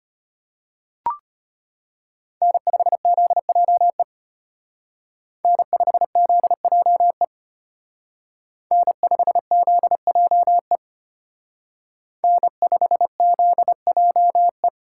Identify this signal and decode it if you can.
Morse code, heard as a keyed tone, about 30 wpm, 700 Hz. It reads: N5ZJE N5ZJE N5ZJE N5ZJE
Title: Speed Racing - USA 1x3 Call Signs 25wpm